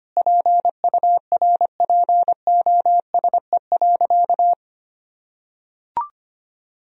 Morse code, heard as a keyed tone, about 25 wpm, 700 Hz. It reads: PURPOSE.
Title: Sentences from Top 1000 Words - Review Part 2 - 25wpm